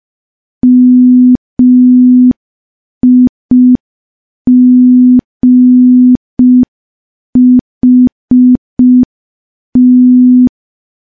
Morse code, heard as a keyed tone, about 5 wpm, 250 Hz. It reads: MIGHT
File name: morse / morse-training-05WPM